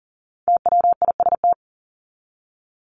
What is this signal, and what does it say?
Morse code, heard as a keyed tone, about 40 wpm, 700 Hz. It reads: TWIST
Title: Top 1751-2000 Words 40wpm